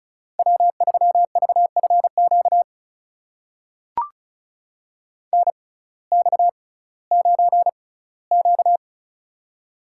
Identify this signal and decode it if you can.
Morse code, heard as a keyed tone, about 36 wpm, spaced as for 19 wpm, 700 Hz. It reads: W3VFQ NX9Q
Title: USA Call Signs - 6x Character Spacing - 35wpm